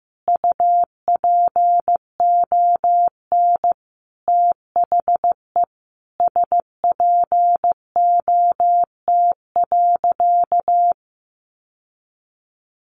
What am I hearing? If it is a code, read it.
Morse code, heard as a keyed tone, about 15 wpm, 700 Hz. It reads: UPON THE SPOT.
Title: Sentences from Top 1100 Words 15wpm